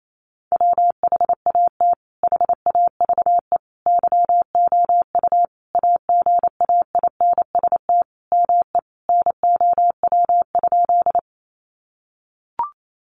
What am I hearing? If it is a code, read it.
Morse code, heard as a keyed tone, about 28 wpm, 700 Hz. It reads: WHAT HAVE YOU AGAINST ME NOW?
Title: Sentences from Top 400 Words 28wpm